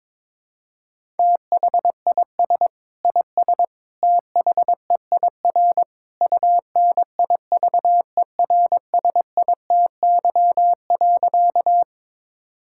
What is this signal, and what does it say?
Morse code, heard as a keyed tone, about 21 wpm, 700 Hz. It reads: THIS IS THEIR UNIVERSITY.
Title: Sentences from Top 1000 Words - Review Part 5 - 22wpm